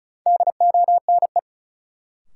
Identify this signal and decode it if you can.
Morse code, heard as a keyed tone, about 36 wpm, 700 Hz. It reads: DONE
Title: Sets of 5 Words Encore - New Word Drill - 35wpm